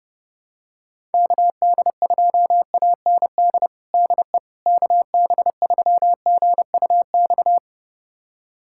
Morse code, heard as a keyed tone, about 32 wpm, 700 Hz. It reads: KD2AND DE KB3GUX
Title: USA Callsign DE Callsign - 6 Characters - 30wpm